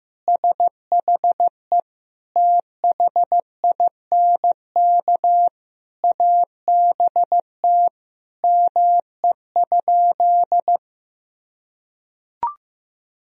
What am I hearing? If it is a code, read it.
Morse code, heard as a keyed tone, about 15 wpm, 700 Hz. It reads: SHE THINK ABT ME?
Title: Sentences from Top 200 Words and Abbreviations 15wpm